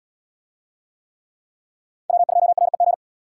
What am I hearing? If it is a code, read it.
Morse code, heard as a keyed tone, about 67 wpm, 700 Hz. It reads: W2REG